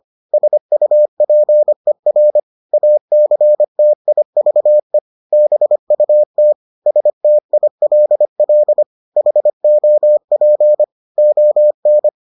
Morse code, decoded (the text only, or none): SUPER ACTIVE BUT STILL HOP ON